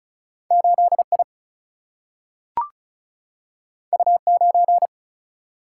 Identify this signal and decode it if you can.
Morse code, heard as a keyed tone, about 36 wpm, 700 Hz. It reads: 8 I U 9
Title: Single-Letter Single-Number 35wpm